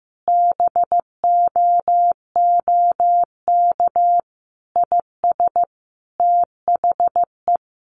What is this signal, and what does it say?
Morse code, heard as a keyed tone, about 15 wpm, 700 Hz. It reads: BOOK IS THE
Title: Sentences from Top 1200 Words 15wpm